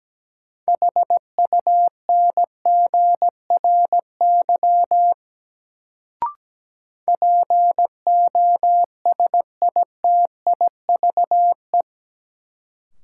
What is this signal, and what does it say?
Morse code, heard as a keyed tone, about 17 wpm, 700 Hz. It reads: HUNGRY POSITIVE
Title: Sentences from Top 1400 Words - New Word Drill Alt - 17wpm